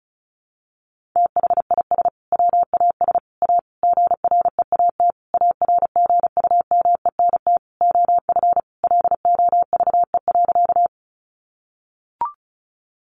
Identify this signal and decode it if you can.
Morse code, heard as a keyed tone, about 34 wpm, 700 Hz. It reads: THIS WAS A GREAT ARGUMENT OF LOVE.
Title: Sentences from Top 1400 Words 35wpm